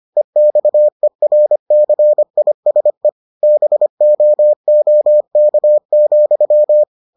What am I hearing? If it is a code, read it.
Morse code, heard as a keyed tone, about 25 wpm, 600 Hz. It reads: EXERCISE BOOK,